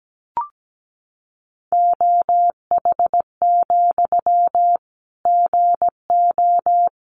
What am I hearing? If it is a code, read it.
Morse code, heard as a keyed tone, about 17 wpm, 700 Hz. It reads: OH, GO
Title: Sentences from Top 500 Words 17wpm